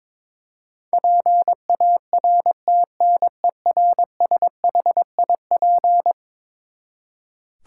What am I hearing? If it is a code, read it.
Morse code, heard as a keyed tone, about 22 wpm, 700 Hz. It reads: PARTNERSHIP
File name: practice / 22wpm / Eleven Letter Words - 22wpm